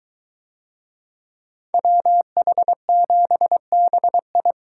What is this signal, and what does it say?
Morse code, heard as a keyed tone, about 22 wpm, 700 Hz. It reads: WH7BI